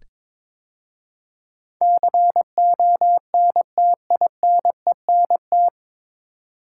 Morse code, heard as a keyed tone, about 22 wpm, 700 Hz. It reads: CONTINENT
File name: 9 Letter Words - First 3 Letters Given - 22wpm